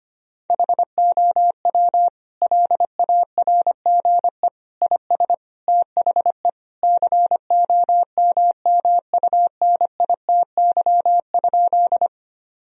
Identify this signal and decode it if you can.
Morse code, heard as a keyed tone, about 25 wpm, 700 Hz. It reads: HOW LARGE IS THE COMMUNITY?
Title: Sentences from Top 1000 Words - Review Part 4 - 25wpm